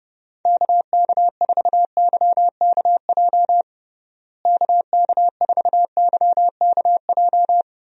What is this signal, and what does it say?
Morse code, heard as a keyed tone, about 30 wpm, 700 Hz. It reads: KK4YKJ KK4YKJ
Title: USA Call Signs - 6 Characters - Repeated 2x - 30wpm